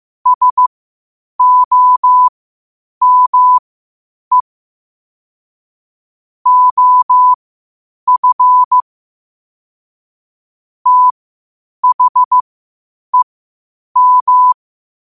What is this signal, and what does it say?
Morse code, heard as a keyed tone, about 15 wpm, spaced as for 8 wpm, 1000 Hz. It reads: SOME OF THEM